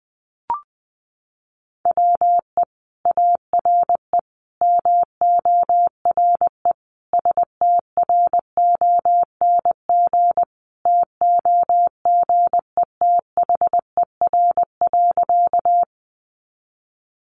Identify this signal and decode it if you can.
Morse code, heard as a keyed tone, about 20 wpm, 700 Hz. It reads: WE ARE MORE STRONG TOGETHER.
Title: Sentences from Top 500 Words 20wpm